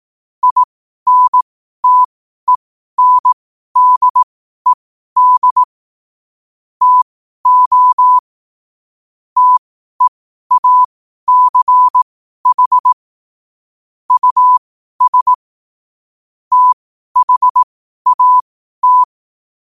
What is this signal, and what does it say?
Morse code, heard as a keyed tone, about 18 wpm, spaced as for 12 wpm, 1000 Hz. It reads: INTENDED TO TEACH US THAT
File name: qotdfiles / 12wpm